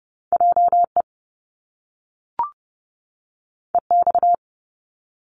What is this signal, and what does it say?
Morse code, heard as a keyed tone, about 30 wpm, 700 Hz. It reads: JE EX
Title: Top Two Letter Combinations in Words 30wpm